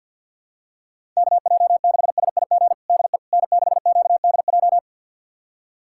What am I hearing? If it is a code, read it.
Morse code, heard as a keyed tone, about 52 wpm, 700 Hz. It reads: KJ6SIG DE N6QDJ
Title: USA Callsign DE Callsign 50wpm